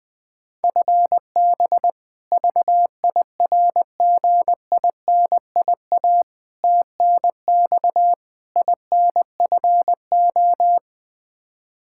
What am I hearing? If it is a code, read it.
Morse code, heard as a keyed tone, about 20 wpm, 700 Hz. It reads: FB VIRGINIA TNX INFO
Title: K1USN SST QSO Practice 20wpm